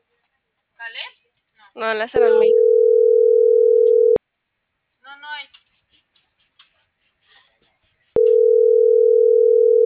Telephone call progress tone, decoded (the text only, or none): ringback tone